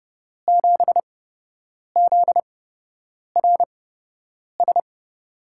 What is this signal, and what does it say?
Morse code, heard as a keyed tone, about 30 wpm, spaced as for 8 wpm, 700 Hz. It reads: 7ZRS
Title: USA Call Signs - 6 Characters - 8x Character Spacing - 30wpm